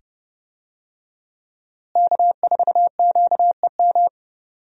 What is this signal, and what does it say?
Morse code, heard as a keyed tone, about 30 wpm, 700 Hz. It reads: K4QEM
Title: USA Call Signs - 5 Characters - Repeated 2x - 30wpm